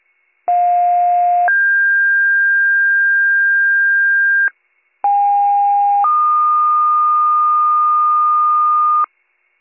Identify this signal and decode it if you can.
Two-tone page, heard as a one-second tone, then a three-second tone, about 710 then 1600 Hz, about 810 then 1220 Hz, with noise. It is heard from 0.5 to 9.0 s.